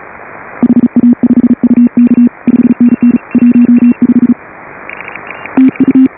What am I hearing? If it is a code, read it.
Morse code, heard as a keyed tone, about 34 wpm, 280 Hz. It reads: HA5UK 5NN 15 TU
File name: AUDIO2018 / HA5BA CW